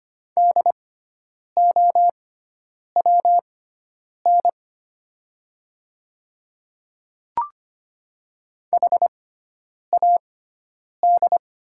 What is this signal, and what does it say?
Morse code, heard as a keyed tone, about 25 wpm, spaced as for 9 wpm, 700 Hz. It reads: DOWN HAD
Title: 6x Character Spacing - Top 100 Words - 25wpm